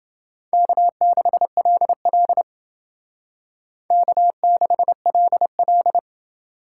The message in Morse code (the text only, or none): K6LL K6LL